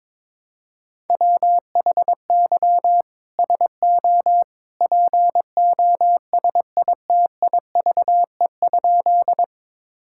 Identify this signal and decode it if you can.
Morse code, heard as a keyed tone, about 22 wpm, 700 Hz. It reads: WHY SO POSITIVE?
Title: Sentences from Top 1000 Words - Review Part 5 - 22wpm